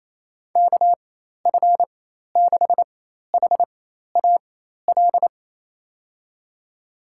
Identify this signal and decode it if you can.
Morse code, heard as a keyed tone, about 28 wpm, spaced as for 13 wpm, 700 Hz. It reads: KF6HAL
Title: USA Call Signs - 6 Characters - 4x Character Spacing - 28wpm